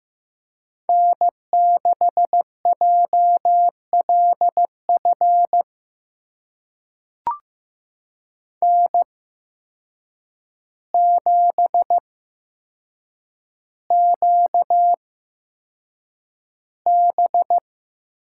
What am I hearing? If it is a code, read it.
Morse code, heard as a keyed tone, about 15 wpm, spaced as for 7 wpm, 700 Hz. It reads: N6JLF N7QB
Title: USA Call Signs - 5 Characters - 8x Character Spacing - 15wpm